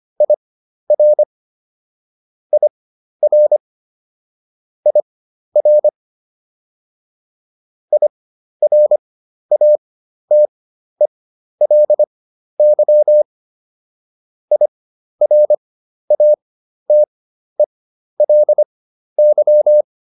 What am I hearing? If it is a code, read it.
Morse code, heard as a keyed tone, about 25 wpm, spaced as for 12 wpm, 600 Hz. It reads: IR IR IR IRATELY IRATELY